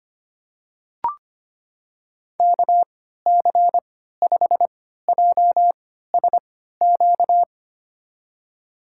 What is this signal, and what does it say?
Morse code, heard as a keyed tone, about 25 wpm, spaced as for 14 wpm, 700 Hz. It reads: KC5JSQ
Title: USA Call Signs - 3x Character Spacing - 25wpm